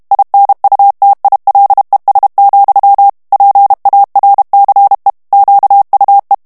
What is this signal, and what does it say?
Morse code, heard as a keyed tone, about 32 wpm, 800 Hz. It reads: INUTILES, PARCE QUE
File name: FRANCAIS TEXTE CLAIR 2 VITESSE 30 MOTS MINUTES